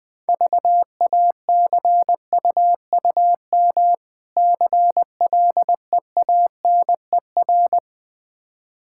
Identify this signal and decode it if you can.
Morse code, heard as a keyed tone, about 20 wpm, 700 Hz. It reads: VACUUM CLEANER
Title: Difficult Words - Speed-Racing with Hints - 20wpm